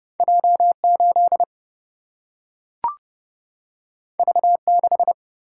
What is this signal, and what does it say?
Morse code, heard as a keyed tone, about 30 wpm, 700 Hz. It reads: J8 V6
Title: Single-Letter Single-Number 30wpm